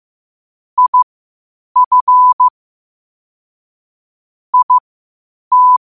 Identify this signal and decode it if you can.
Morse code, heard as a keyed tone, about 15 wpm, spaced as for 8 wpm, 1000 Hz. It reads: IF IT
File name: qotdfiles / 8wpm